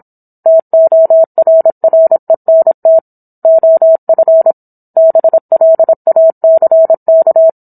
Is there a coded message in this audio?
Yes: Morse code from 0.5 to 7.5 s, heard as a keyed tone, about 27 wpm, 650 Hz. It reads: TORRENT OF BLACK